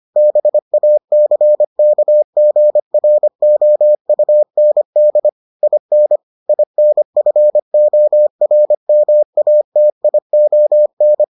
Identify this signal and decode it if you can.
Morse code, heard as a keyed tone, about 25 wpm, 600 Hz. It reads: BACKGROUND IN INFORMATION